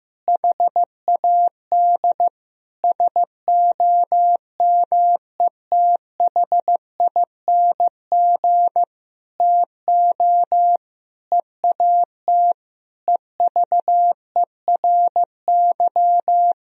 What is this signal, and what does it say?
Morse code, heard as a keyed tone, about 15 wpm, 700 Hz. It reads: HAD SOMETHING TO EAT EVERY
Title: Sentences from Top 1000 Words - Review Part 4 - 15wpm